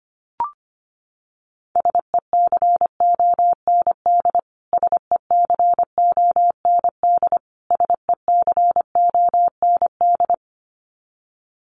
Morse code, heard as a keyed tone, about 25 wpm, 700 Hz. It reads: SECOND SECOND SECOND
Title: Top 300 Words Repeated 25wpm